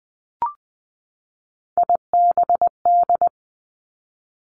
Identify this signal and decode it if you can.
Morse code, heard as a keyed tone, about 20 wpm, 700 Hz. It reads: IBD
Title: Three Challenging Letter-Number Combinations 20wpm